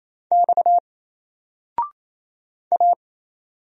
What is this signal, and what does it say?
Morse code, heard as a keyed tone, about 27 wpm, 700 Hz. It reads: X A
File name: Single-Letter-Number-28wpm